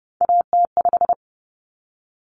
Morse code, heard as a keyed tone, about 30 wpm, 700 Hz. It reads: AT5
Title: Common Cut Numbers - 3-Digit 30wpm